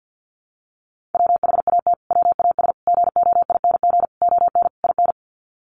Morse code, heard as a keyed tone, about 50 wpm, 700 Hz. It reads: WHAT WAS GOING ON IN
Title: Sets of 5 Words Encore 50wpm